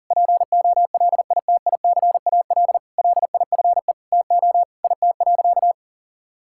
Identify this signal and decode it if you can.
Morse code, heard as a keyed tone, about 37 wpm, 700 Hz. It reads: POLITICAL LIFE TO IT.
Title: Sentences from Top 1000 Words - Review Part 5 - 40wpm